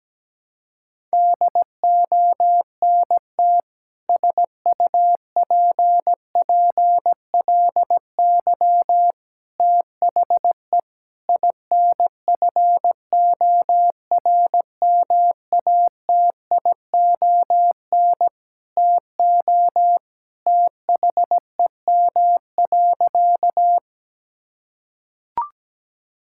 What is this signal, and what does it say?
Morse code, heard as a keyed tone, about 17 wpm, 700 Hz. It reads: DONT SUPPLY THE INFORMATION TO THEM.
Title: Sentences from Top 1000 Words - Review Part 5 - 17wpm